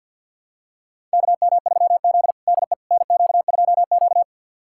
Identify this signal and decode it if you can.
Morse code, heard as a keyed tone, about 51 wpm, 700 Hz. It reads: KM3Z DE NQ2Q
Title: USA Callsign DE Callsign - 4 Characters - 50wpm